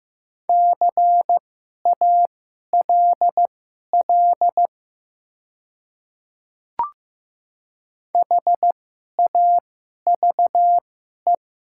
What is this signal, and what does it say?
Morse code, heard as a keyed tone, about 15 wpm, spaced as for 11 wpm, 700 Hz. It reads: CALL HAVE
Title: 2x Character Spacing - Top 500 Words - 15wpm